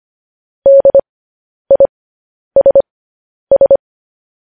Morse code, heard as a keyed tone, about 25 wpm, spaced as for 10 wpm, 550 Hz. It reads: DISS